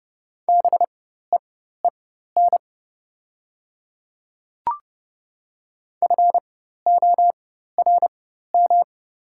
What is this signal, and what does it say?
Morse code, heard as a keyed tone, about 30 wpm, spaced as for 14 wpm, 700 Hz. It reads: BEEN FORM
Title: 4x Character Spacing - Top 300 Words - 30wpm